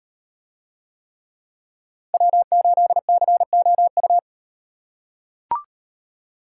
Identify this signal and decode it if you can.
Morse code, heard as a keyed tone, about 38 wpm, 700 Hz. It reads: W8COU